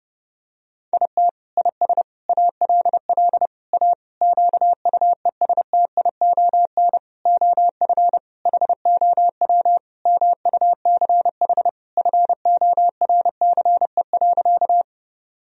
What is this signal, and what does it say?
Morse code, heard as a keyed tone, about 32 wpm, 700 Hz. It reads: IT IS ALL A QUESTION OF HOW MUCH FORCE.
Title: Sentences from Top 1000 Words - Review Part 3 - 30wpm